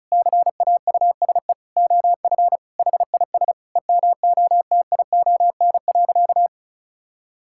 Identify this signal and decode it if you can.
Morse code, heard as a keyed tone, about 34 wpm, 700 Hz. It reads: CAUSE OF HIS EMOTION.
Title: Sentences from Top 1600 Words 35wpm